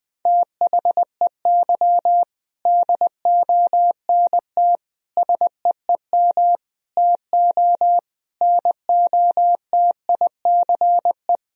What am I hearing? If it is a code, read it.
Morse code, heard as a keyed tone, about 20 wpm, 700 Hz. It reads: THEY DONT SEEM TO NOTICE
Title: Sentences from Top 1000 Words - Review Part 2 - 20wpm